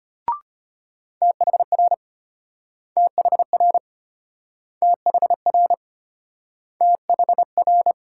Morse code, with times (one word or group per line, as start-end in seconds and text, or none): 1.2-1.9 THR
3.0-3.8 THR
4.8-5.7 THR
6.8-7.9 THR